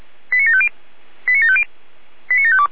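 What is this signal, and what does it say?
Five-tone selective call (ZVEI), heard as quick runs of five tones, tones of about 70 ms. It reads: 89750 89750 89752
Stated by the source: ZVEI five-tone selective call received by SDR.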